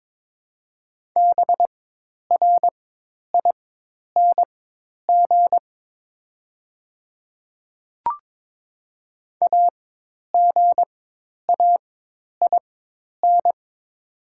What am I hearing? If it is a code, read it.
Morse code, heard as a keyed tone, about 22 wpm, spaced as for 10 wpm, 700 Hz. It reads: BRING AGAIN
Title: 4x Character Spacing - Top 500 Words - 22wpm